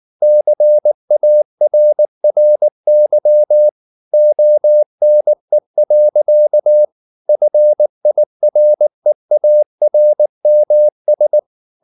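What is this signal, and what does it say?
Morse code, heard as a keyed tone, about 19 wpm, 600 Hz. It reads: CARRY ONE. FIREARMS
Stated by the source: Morse code at 19 wpm from CW med Gnister, reading Ken Follett's novel Never in English.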